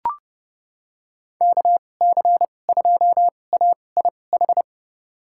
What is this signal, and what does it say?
Morse code, heard as a keyed tone, about 30 wpm, spaced as for 22 wpm, 700 Hz. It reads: KC2AIH
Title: USA Call Signs - 6 Characters - 2x Character Spacing - 30wpm